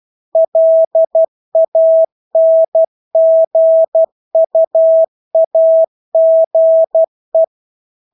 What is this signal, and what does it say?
Morse code, heard as a keyed tone, about 12 wpm, 650 Hz. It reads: LANGUAGE
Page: Engelske ord 12 wpm